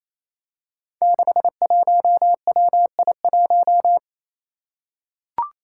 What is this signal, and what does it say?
Morse code, heard as a keyed tone, about 28 wpm, 700 Hz. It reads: 61WI1
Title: Five Challenging Letter-Number Combinations 28wpm